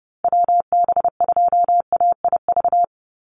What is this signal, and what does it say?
Morse code, heard as a keyed tone, about 30 wpm, 700 Hz. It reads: WB2AIV
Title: Call Signs Repeated 3x - 30wpm